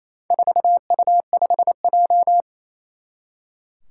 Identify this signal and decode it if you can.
Morse code, heard as a keyed tone, about 28 wpm, 700 Hz. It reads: 4U5J
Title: Four Challenging Letter-Number Combinations 28wpm